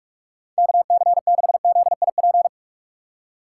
Morse code, heard as a keyed tone, about 58 wpm, spaced as for 51 wpm, 700 Hz. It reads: KC6ZIP
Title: Speed Racing - USA 2x3 Call Signs 45wpm